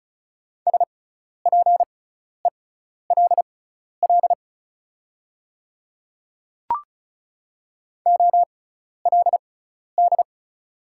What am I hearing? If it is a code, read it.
Morse code, heard as a keyed tone, about 34 wpm, spaced as for 12 wpm, 700 Hz. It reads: SPELL OLD
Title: 6x Character Spacing - Top 500 Words - 35wpm